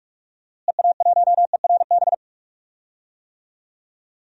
Morse code, heard as a keyed tone, about 46 wpm, 700 Hz. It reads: EA1ERB
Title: DXCC-100 Call Signs - 6 Characters - Repeated 3x - 45wpm